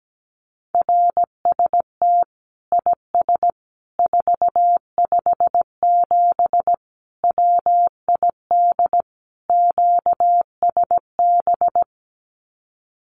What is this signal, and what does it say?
Morse code, heard as a keyed tone, about 17 wpm, 700 Hz. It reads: RST IS 457 WID QSB
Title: Common QSO Phrases 17wpm